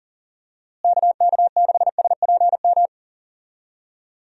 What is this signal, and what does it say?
Morse code, heard as a keyed tone, about 40 wpm, 700 Hz. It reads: KK6SPM
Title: USA Call Signs - 6 Characters - Repeated 2x - 40wpm